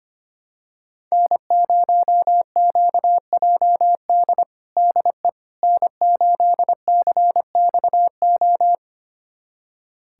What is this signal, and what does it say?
Morse code, heard as a keyed tone, about 25 wpm, 700 Hz. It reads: N0QJD DE N8CXO
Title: USA Callsign DE Callsign - 5 Characters - 25wpm